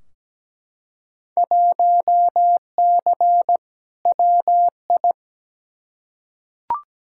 Morse code, heard as a keyed tone, about 17 wpm, 700 Hz. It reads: 1C WI
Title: Field Day Exchanges - 17wpm